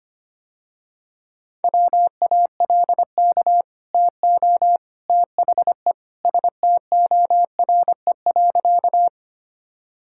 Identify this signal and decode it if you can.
Morse code, heard as a keyed tone, about 25 wpm, 700 Hz. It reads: WALK TO THE STORE.
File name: Easy Sentences with Hints - 25wpm